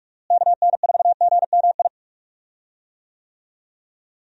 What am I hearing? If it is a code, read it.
Morse code, heard as a keyed tone, about 46 wpm, 700 Hz. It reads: KN4GMI
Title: USA Call Signs - 6 Characters - Repeated 3x - 45wpm